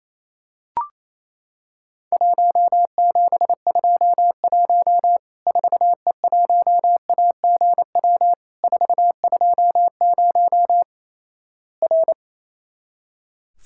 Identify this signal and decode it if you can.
Morse code, heard as a keyed tone, about 28 wpm, 700 Hz. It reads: 1721 4E1AGW 420 R
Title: Worked All Europe DX Contest - QTC Exchange - 28wpm